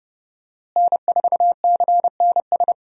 Morse code, heard as a keyed tone, about 30 wpm, 700 Hz. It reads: N4CNS
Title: USA Call Signs - 5 Characters - Repeated 2x - 30wpm